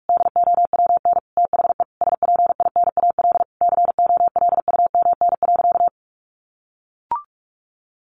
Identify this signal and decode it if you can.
Morse code, heard as a keyed tone, about 44 wpm, 700 Hz. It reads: DOWN THE SPINAL COLUMN.
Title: A Sound Of Thunder 45wpm